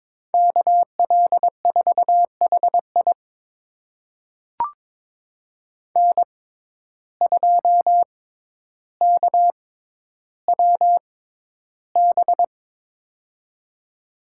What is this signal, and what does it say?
Morse code, heard as a keyed tone, about 22 wpm, spaced as for 11 wpm, 700 Hz. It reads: KL4HI N2KWB
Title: USA Call Signs - 6x Character Spacing - 22wpm